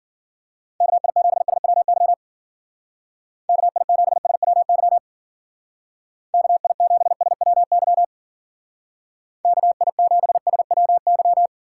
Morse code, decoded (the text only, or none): KI7SWY KI7SWY KI7SWY KI7SWY